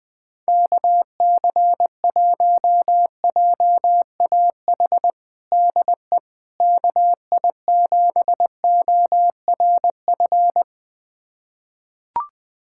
Morse code, heard as a keyed tone, about 20 wpm, 700 Hz. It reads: KC1JAH DE KI7ORF
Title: USA Callsign DE Callsign 20wpm